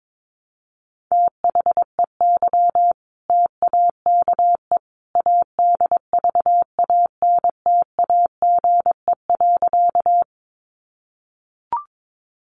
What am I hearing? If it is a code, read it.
Morse code, heard as a keyed tone, about 22 wpm, 700 Hz. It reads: THEY TAKE ADVANTAGE.
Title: Sentences from Top 900 Words 22wpm